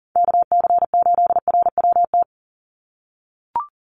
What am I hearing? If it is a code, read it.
Morse code, heard as a keyed tone, about 40 wpm, 700 Hz. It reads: KC8RWT
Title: USA Call Signs 40wpm